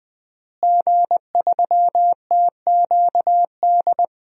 Morse code, heard as a keyed tone, about 21 wpm, 700 Hz. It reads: G3TQD
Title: DXCC-100 Call Signs - 5 Characters - Repeated 3x - 20wpm